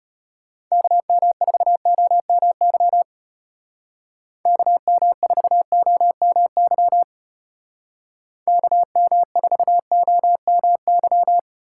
Morse code, heard as a keyed tone, about 33 wpm, 700 Hz. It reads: KM4OMY KM4OMY KM4OMY